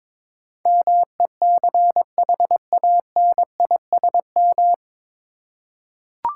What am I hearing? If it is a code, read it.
Morse code, heard as a keyed tone, about 22 wpm, 700 Hz. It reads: MECHANISM
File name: 9 Letter Words - First 3 Letters Given - 22wpm